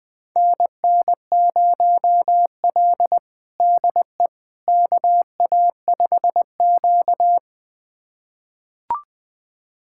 Morse code, heard as a keyed tone, about 21 wpm, 700 Hz. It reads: NN0L DE KA5Q
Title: USA Callsign DE Callsign - 4 Characters - 20wpm